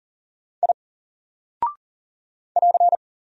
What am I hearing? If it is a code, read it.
Morse code, heard as a keyed tone, about 40 wpm, 700 Hz. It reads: I <AR>
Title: Single Character (Letter:Number:Punctuation:Prosign) Alt 40wpm